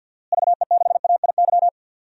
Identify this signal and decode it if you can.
Morse code, heard as a keyed tone, about 50 wpm, 700 Hz. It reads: VE6AIY
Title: DXCC-100 Call Signs - 6 Characters - Repeated 3x - 50wpm